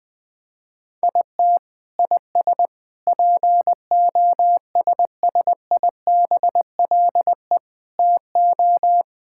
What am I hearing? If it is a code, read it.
Morse code, heard as a keyed tone, about 20 wpm, 700 Hz. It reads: IT IS POSSIBLE TO
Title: Sentences from Top 1000 Words - Review Part 6 - 20wpm